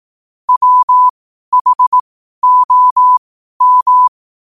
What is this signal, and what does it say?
Morse code, heard as a keyed tone, about 18 wpm, spaced as for 12 wpm, 1000 Hz. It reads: WHOM
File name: qotdfiles / 12wpm